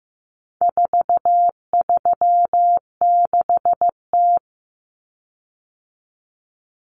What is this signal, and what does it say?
Morse code, heard as a keyed tone, about 15 wpm, 700 Hz. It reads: 436T
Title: Common Cut Numbers - 4-Digit 15wpm